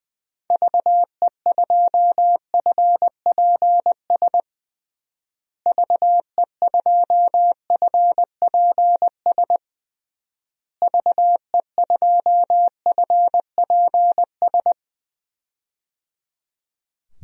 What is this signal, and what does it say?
Morse code, heard as a keyed tone, about 20 wpm, 700 Hz. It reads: VE2FPS VE2FPS VE2FPS
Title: DXCC-100 Call Signs - 6 Characters - Repeated 3x - 20wpm